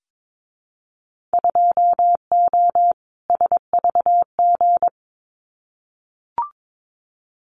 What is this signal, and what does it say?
Morse code, heard as a keyed tone, about 22 wpm, 700 Hz. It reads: 2O SVG
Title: Winter Field Day Exchanges 22wpm